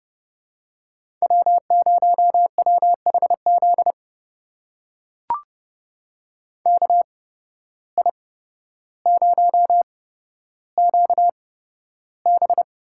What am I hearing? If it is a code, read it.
Morse code, heard as a keyed tone, about 30 wpm, spaced as for 13 wpm, 700 Hz. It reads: W0WHZ KI0QB
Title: USA Call Signs - 5 Characters - 8x Character Spacing - 30wpm